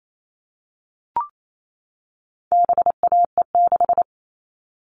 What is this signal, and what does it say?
Morse code, heard as a keyed tone, about 28 wpm, 700 Hz. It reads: BAE6
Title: All Cut Numbers - 4-Digit 28wpm